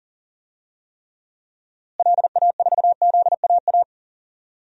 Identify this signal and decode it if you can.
Morse code, heard as a keyed tone, about 40 wpm, 700 Hz. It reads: LA4ZAA
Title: DXCC-100 Call Signs - 6 Characters - Repeated 2x - 40wpm